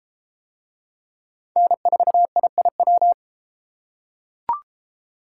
Morse code, heard as a keyed tone, about 30 wpm, 700 Hz. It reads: N4IIW